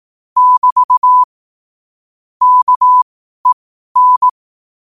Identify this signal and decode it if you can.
Morse code, heard as a keyed tone, about 18 wpm, spaced as for 12 wpm, 1000 Hz. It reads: <BT> KEN
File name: qotdfiles / 12wpm